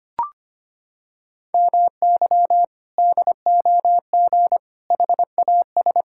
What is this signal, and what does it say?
Morse code, heard as a keyed tone, about 25 wpm, 700 Hz. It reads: MY DOG HAS
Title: Sentences from Top 1600 Words 25wpm